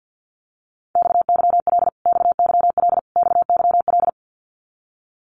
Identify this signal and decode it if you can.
Morse code, heard as a keyed tone, about 50 wpm, 700 Hz. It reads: XYL XYL XYL
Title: Abbreviations Repeated 3x - 50wpm